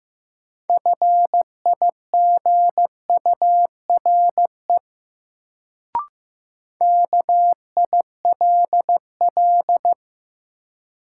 Morse code, heard as a keyed tone, about 15 wpm, 700 Hz. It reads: FIGURE KILL
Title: Sentences from Top 1400 Words - New Word Drill Alt - 15wpm